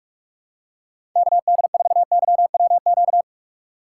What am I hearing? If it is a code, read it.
Morse code, heard as a keyed tone, about 46 wpm, 700 Hz. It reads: KD4YWQ
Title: USA Call Signs - 6 Characters - 6x Character Spacing - 45wpm